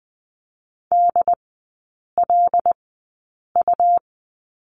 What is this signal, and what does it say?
Morse code, heard as a keyed tone, about 20 wpm, spaced as for 8 wpm, 700 Hz. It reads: DLU
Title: 121 Lesson 20e - Number 3 - Three Characters 20wpm